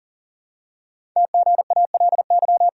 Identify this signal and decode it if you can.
Morse code, heard as a keyed tone, about 39 wpm, 700 Hz. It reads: TGALY
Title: Code Groups - 5 Letters - 40wpm